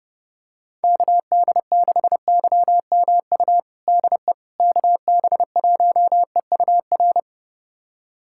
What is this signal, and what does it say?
Morse code, heard as a keyed tone, about 29 wpm, 700 Hz. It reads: KD6YMU DE KB1EUR
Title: USA Callsign DE Callsign - 6 Characters - 30wpm